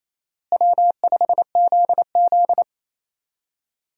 Morse code, heard as a keyed tone, about 28 wpm, 700 Hz. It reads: W5ZZ
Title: USA Call Signs - 4 Characters - Repeated 3x - 28wpm